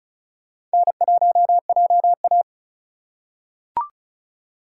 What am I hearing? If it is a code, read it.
Morse code, heard as a keyed tone, about 36 wpm, 700 Hz. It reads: N1JA
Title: USA Call Signs - 4 Characters - 3x Character Spacing - 35wpm